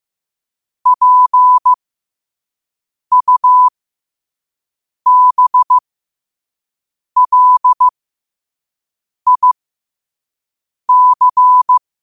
Morse code, heard as a keyed tone, about 15 wpm, spaced as for 5 wpm, 1000 Hz. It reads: PUBLIC
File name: qotdfiles / 5wpm